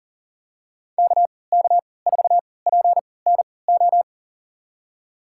Morse code, heard as a keyed tone, about 40 wpm, spaced as for 23 wpm, 700 Hz. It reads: KK4PNO
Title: USA Call Signs - 6 Characters - 3x Character Spacing - 40wpm